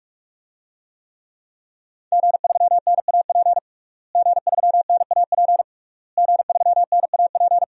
Morse code, heard as a keyed tone, about 46 wpm, 700 Hz. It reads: G3NAP G3NAP G3NAP
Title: DXCC-100 Call Signs - 5 Characters - Repeated 3x - 45wpm